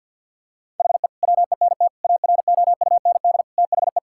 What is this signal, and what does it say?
Morse code, heard as a keyed tone, about 50 wpm, 700 Hz. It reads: HE WENT AROUND THE
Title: Sentences from Top 1000 Words - Review Part 5 - 50wpm